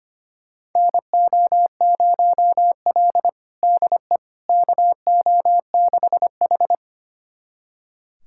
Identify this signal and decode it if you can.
Morse code, heard as a keyed tone, about 25 wpm, 700 Hz. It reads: NO0L DE KO6H
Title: USA Callsign DE Callsign - 4 Characters - 25wpm